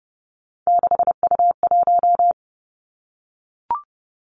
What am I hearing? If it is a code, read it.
Morse code, heard as a keyed tone, about 30 wpm, 700 Hz. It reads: 6U1
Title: Three Challenging Letter-Number Combinations 30wpm